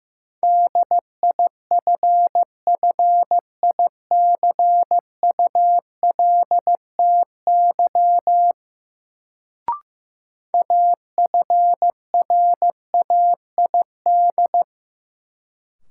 Morse code, heard as a keyed tone, about 15 wpm, 700 Hz. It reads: DIFFICULTY AFRAID
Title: Sentences from Top 1500 Words - New Word Drill Alt - 15wpm